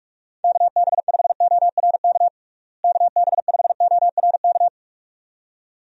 Morse code, heard as a keyed tone, about 46 wpm, 700 Hz. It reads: KB5ORK KB5ORK
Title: USA Call Signs - 6 Characters - Repeated 2x - 45wpm